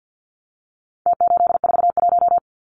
Morse code, heard as a keyed tone, about 49 wpm, 700 Hz. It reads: T841
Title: Numbers - Contest Style - 4-Digit 50wpm